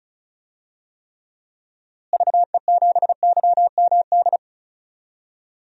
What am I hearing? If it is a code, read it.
Morse code, heard as a keyed tone, about 34 wpm, 700 Hz. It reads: VE7YMD
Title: DXCC-100 Call Signs - 6 Characters - Repeated 2x - 35wpm